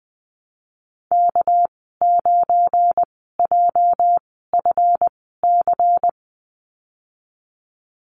Morse code, heard as a keyed tone, about 20 wpm, spaced as for 14 wpm, 700 Hz. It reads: K9JFC
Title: 238 Lesson 38g - Letter q - Call Signs 20wpm